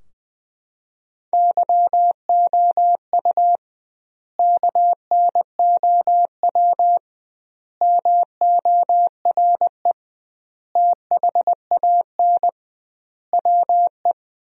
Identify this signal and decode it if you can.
Morse code, heard as a keyed tone, about 21 wpm, 700 Hz. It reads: YOU KNOW MORE THAN WE
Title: 2x Word-Spacing - Sentences from Top 100 Words - 20wpm